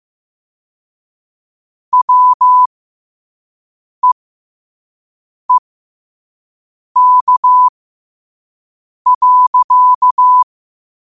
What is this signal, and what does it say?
Morse code, heard as a keyed tone, about 15 wpm, spaced as for 5 wpm, 1000 Hz. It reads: WEEK.